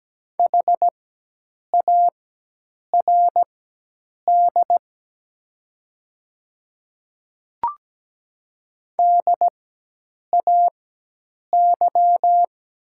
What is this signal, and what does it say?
Morse code, heard as a keyed tone, about 17 wpm, spaced as for 8 wpm, 700 Hz. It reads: HARD DAY
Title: 4x Character Spacing - Top 300 Words - 17wpm